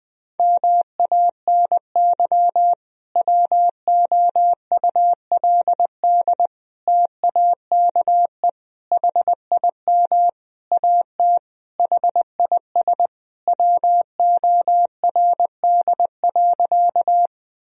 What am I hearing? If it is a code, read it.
Morse code, heard as a keyed tone, about 20 wpm, 700 Hz. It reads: MANY WOULD TAKE HIM AT HIS WORD.
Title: Sentences from Top 1000 Words - Review Part 1 - 20wpm